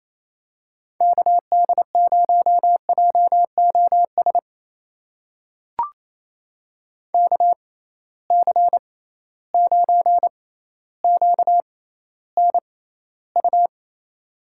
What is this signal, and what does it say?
Morse code, heard as a keyed tone, about 28 wpm, spaced as for 14 wpm, 700 Hz. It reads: KD0JOS KC9QNU
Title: USA Call Signs - 6 Characters - 6x Character Spacing - 28wpm